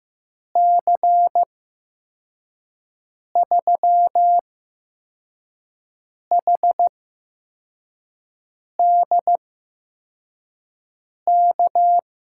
Morse code, heard as a keyed tone, about 15 wpm, spaced as for 4 wpm, 700 Hz. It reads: C3HDK
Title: USA Call Signs - 6 Characters - 8x Character Spacing - 15wpm